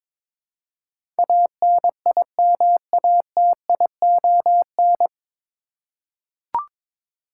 Morse code, heard as a keyed tone, about 21 wpm, 700 Hz. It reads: ANIMATION
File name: 9 Letter Words - First 3 Letters Given - 22wpm